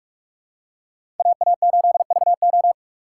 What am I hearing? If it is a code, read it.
Morse code, heard as a keyed tone, about 45 wpm, 700 Hz. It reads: AA8VO
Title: USA Call Signs - 5 Characters - 4x Character Spacing - 45wpm